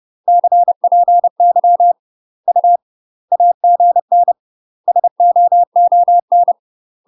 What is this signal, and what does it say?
Morse code, heard as a keyed tone, about 30 wpm, 700 Hz. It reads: CPY U AGN SOON
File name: UK MASTER QSOs TRAINING / 30WPM / UK MASTER QSOs 0050